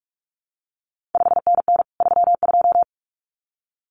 Morse code, heard as a keyed tone, about 46 wpm, 700 Hz. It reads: 5NN 32
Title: CQ WW DX Contest Exchange Encore - 45wpm